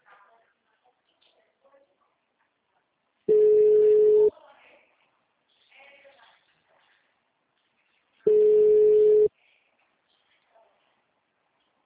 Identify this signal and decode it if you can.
Ringback tone (Europe, 425 Hz).
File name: out- -2044- - - .9821